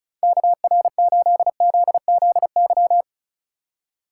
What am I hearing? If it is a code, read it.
Morse code, heard as a keyed tone, about 36 wpm, 700 Hz. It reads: KR8ZZY
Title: USA Call Signs - 6 Characters - 3x Character Spacing - 35wpm